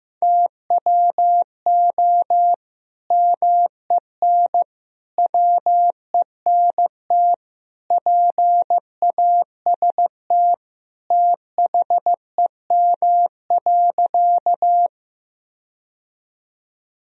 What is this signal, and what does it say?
Morse code, heard as a keyed tone, about 15 wpm, 700 Hz. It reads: TWO MEN WENT PAST THEM.
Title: Sentences from Top 1000 Words - Review Part 3 - 15wpm